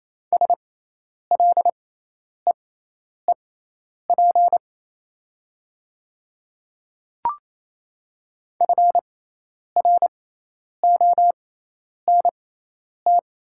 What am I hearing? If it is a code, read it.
Morse code, heard as a keyed tone, about 27 wpm, spaced as for 10 wpm, 690 Hz. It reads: SLEEP FRONT
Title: 6x Character Spacing - Top 500 Words - 28wpm